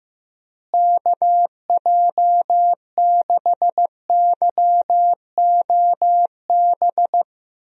Morse code, heard as a keyed tone, about 15 wpm, 700 Hz. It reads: KJ6YOB
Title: USA Call Signs - 6 Characters - Repeated 2x - 15wpm